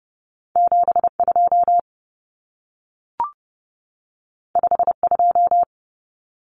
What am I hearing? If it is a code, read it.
Morse code, heard as a keyed tone, about 30 wpm, 700 Hz. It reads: 72 52
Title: Common Cut Numbers - 2-Digit 30wpm